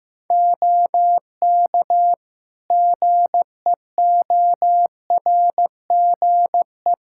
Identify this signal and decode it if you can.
Morse code, heard as a keyed tone, about 15 wpm, 700 Hz. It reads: OK GEORGE
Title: K1USN SST QSO Practice 15wpm